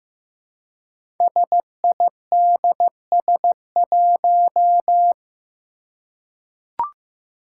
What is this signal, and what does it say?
Morse code, heard as a keyed tone, about 15 wpm, 700 Hz. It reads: SIDS1
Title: Five Challenging Letter-Number Combinations 15wpm